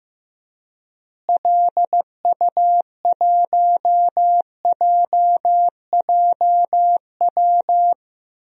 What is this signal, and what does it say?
Morse code, heard as a keyed tone, about 15 wpm, 700 Hz. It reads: LU1JJW
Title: DXCC-100 Call Signs - 6 Characters - Repeated 3x - 15wpm